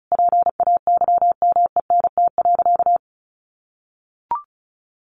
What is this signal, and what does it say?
Morse code, heard as a keyed tone, about 34 wpm, 700 Hz. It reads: PAYMENT.
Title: Sentences from Top 1400 Words 35wpm